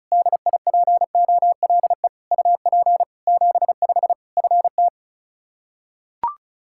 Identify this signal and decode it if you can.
Morse code, heard as a keyed tone, about 34 wpm, 700 Hz. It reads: DIPOLE UP 75 FT
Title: Practice QSO - Exchange 2 - Info - 35wpm